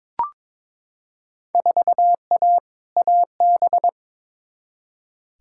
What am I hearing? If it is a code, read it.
Morse code, heard as a keyed tone, about 22 wpm, 700 Hz. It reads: 4A AB
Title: Field Day Exchanges Encore - 22wpm